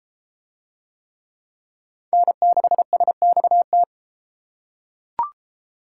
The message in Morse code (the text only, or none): N6SXT